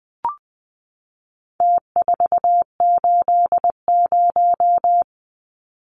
Morse code, heard as a keyed tone, about 20 wpm, 700 Hz. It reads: T480
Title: Numbers - Contest Style - 4-Digit 20wpm